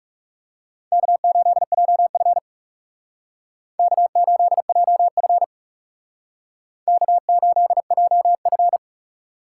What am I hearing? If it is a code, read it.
Morse code, heard as a keyed tone, about 38 wpm, 700 Hz. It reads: K8JF K8JF K8JF